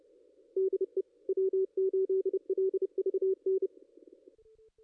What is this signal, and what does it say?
Morse code, heard as a keyed tone, about 28 wpm, 380 Hz. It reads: DE W8LVN
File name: contests / ss-cw-k5tr-op-n5rz -2200